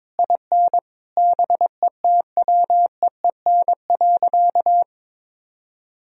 Morse code, heard as a keyed tone, about 22 wpm, 700 Hz. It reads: IN BETWEEN.
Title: Sentences from Top 1000 Words - Review Part 1 - 22wpm